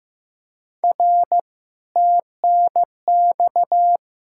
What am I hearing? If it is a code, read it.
Morse code, heard as a keyed tone, about 15 wpm, 700 Hz. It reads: R TNX
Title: Speed-Racing Common QSO Phrases Encore - 15wpm